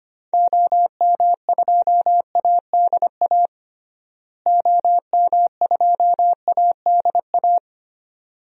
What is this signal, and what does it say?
Morse code, heard as a keyed tone, about 25 wpm, 700 Hz. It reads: OM2ADA OM2ADA
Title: DXCC-100 Call Signs - 6 Characters - Repeated 3x - 25wpm